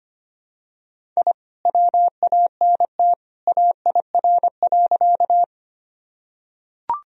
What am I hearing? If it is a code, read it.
Morse code, heard as a keyed tone, about 25 wpm, 700 Hz. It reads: I WANT AIR.
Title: Sentences from Top 1000 Words - Review Part 1 - 25wpm